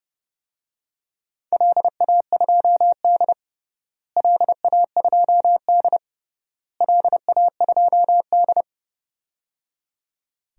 Morse code, heard as a keyed tone, about 30 wpm, 700 Hz. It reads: LA2D LA2D LA2D
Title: DXCC-100 Call Signs - 4 Characters - Repeated 3x - 30wpm